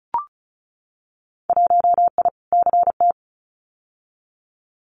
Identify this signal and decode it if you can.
Morse code, heard as a keyed tone, about 34 wpm, 700 Hz. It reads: 1I CT
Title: Winter Field Day Exchanges 35wpm